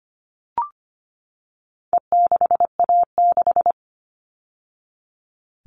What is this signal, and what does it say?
Morse code, heard as a keyed tone, about 25 wpm, 700 Hz. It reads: E6A6
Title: All Cut Numbers - 4-Digit 25wpm